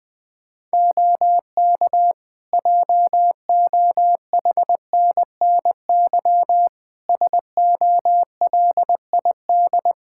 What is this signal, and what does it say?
Morse code, heard as a keyed tone, about 21 wpm, 700 Hz. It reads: OK JOHNNY SOLID
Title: Practice QSO - Exchange 3 - More Info - 20wpm